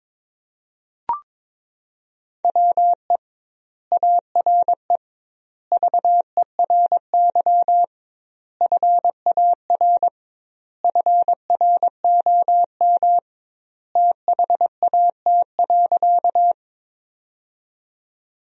Morse code, heard as a keyed tone, about 22 wpm, 700 Hz. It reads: WE ARE VERY FAR FROM THAT.
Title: 2x Word-Spacing - Sentences from Top 300 Words - 22wpm